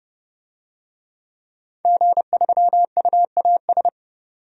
Morse code, heard as a keyed tone, about 31 wpm, 700 Hz. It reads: G3UAS
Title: DXCC-100 Call Signs - 5 Characters - 30wpm